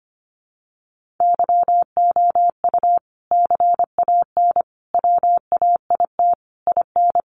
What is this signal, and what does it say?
Morse code, heard as a keyed tone, about 25 wpm, 700 Hz. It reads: YOU CAN WAIT IN
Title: Sentences from Top 1400 Words 25wpm